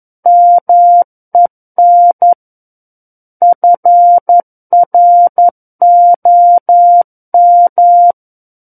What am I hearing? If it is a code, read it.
Morse code, heard as a keyed tone, about 11 wpm, 700 Hz. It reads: MEN FROM
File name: 11 WPM